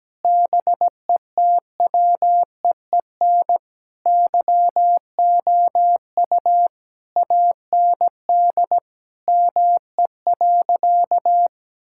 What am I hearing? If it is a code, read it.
Morse code, heard as a keyed tone, about 17 wpm, 700 Hz. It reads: BETWEEN YOU AND ME.
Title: Sentences from Top 1000 Words - Review Part 6 - 17wpm